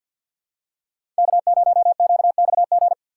Morse code, heard as a keyed tone, about 53 wpm, 700 Hz. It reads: K0QXG
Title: USA Call Signs - 3x Character Spacing - 50wpm